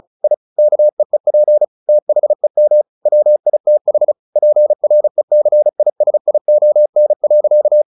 Morse code, heard as a keyed tone, about 34 wpm, 600 Hz. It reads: I KEEP THEM WITH PRECISION.
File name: the old man and the sea / the old man and the sea 1 35wpm